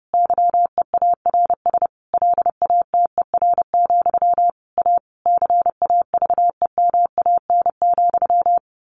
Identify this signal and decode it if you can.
Morse code, heard as a keyed tone, about 30 wpm, 700 Hz. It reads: YEARS LATER, A CAVEMAN,
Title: A Sound Of Thunder 30wpm